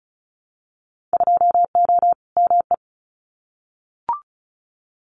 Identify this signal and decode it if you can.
Morse code, heard as a keyed tone, about 34 wpm, 690 Hz. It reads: 2O ME
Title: Winter Field Day Exchanges 35wpm